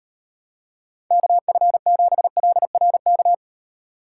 Morse code, heard as a keyed tone, about 38 wpm, 700 Hz. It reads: KF7LRK